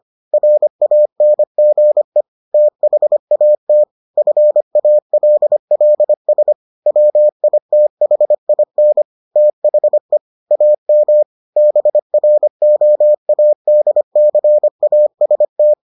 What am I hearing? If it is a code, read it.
Morse code, heard as a keyed tone, about 25 wpm, 600 Hz. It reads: RANGE THAT FALLS WITHIN THE AM BROADCAST